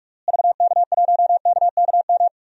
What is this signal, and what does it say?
Morse code, heard as a keyed tone, about 41 wpm, 700 Hz. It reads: VK1KKM
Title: DXCC-100 Call Signs - 6 Characters - Repeated 3x - 45wpm